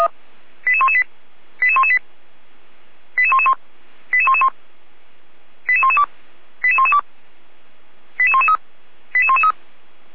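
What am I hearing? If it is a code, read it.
Five-tone selective call (ZVEI), heard as quick runs of five tones, tones of about 70 ms. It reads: 88118 88118 88111 88111 88112 88112 88113 88113
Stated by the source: Five-tone ZVEI selective call received on SDR.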